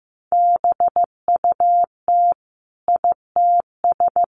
Morse code, heard as a keyed tone, about 15 wpm, 700 Hz. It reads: BUT ITS
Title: Sentences from Top 700 Words 15wpm